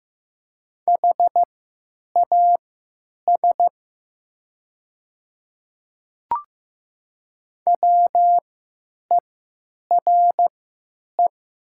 Morse code, heard as a keyed tone, about 15 wpm, spaced as for 9 wpm, 700 Hz. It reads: HAS WERE
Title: 3x Character Spacing - Top 300 Words - 15wpm